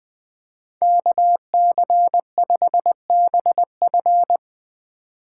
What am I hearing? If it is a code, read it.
Morse code, heard as a keyed tone, about 20 wpm, 700 Hz. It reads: KC5BF
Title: Speed Racing - USA 2x2 Call Signs 17wpm